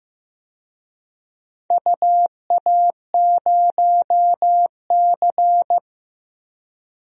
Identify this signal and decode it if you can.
Morse code, heard as a keyed tone, about 15 wpm, 700 Hz. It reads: UA0C
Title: DXCC-100 Call Signs - 4 Characters - Repeated 2x - 15wpm